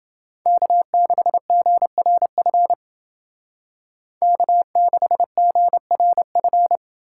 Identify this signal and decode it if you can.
Morse code, heard as a keyed tone, about 28 wpm, 700 Hz. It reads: K6GRF K6GRF